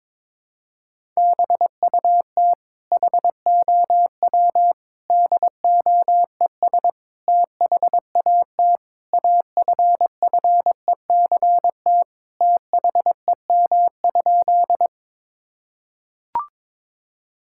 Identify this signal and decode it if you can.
Morse code, heard as a keyed tone, about 22 wpm, 700 Hz. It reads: BUT HOW DOES THAT AFFECT THEM?
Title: Sentences from Top 1000 Words - Review Part 6 - 22wpm